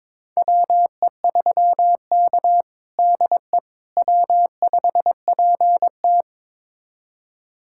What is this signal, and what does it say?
Morse code, heard as a keyed tone, about 22 wpm, 700 Hz. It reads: WE3K DE W5PT
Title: USA Callsign DE Callsign - 4 Characters - 22wpm